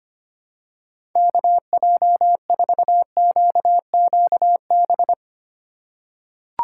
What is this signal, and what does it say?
Morse code, heard as a keyed tone, about 25 wpm, 700 Hz. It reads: KJ4QQB
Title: USA Call Signs - 2x Character Spacing - 25wpm